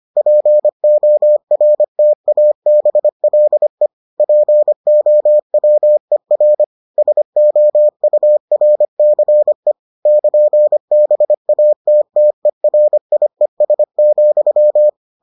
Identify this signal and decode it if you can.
Morse code, heard as a keyed tone, about 25 wpm, 600 Hz. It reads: PORTABLE POWER SOURCE <KN>BATTERIES,